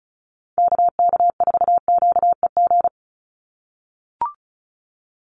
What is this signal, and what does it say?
Morse code, heard as a keyed tone, about 34 wpm, 700 Hz. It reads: KK4QEG
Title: USA Call Signs 35wpm